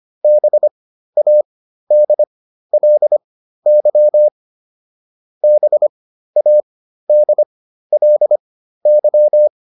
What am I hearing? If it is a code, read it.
Morse code, heard as a keyed tone, about 25 wpm, spaced as for 13 wpm, 600 Hz. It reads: BADLY BADLY